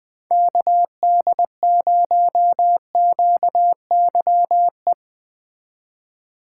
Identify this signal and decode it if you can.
Morse code, heard as a keyed tone, about 21 wpm, 700 Hz. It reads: KD0QYE
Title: USA Call Signs - 6 Characters - 4x Character Spacing - 20wpm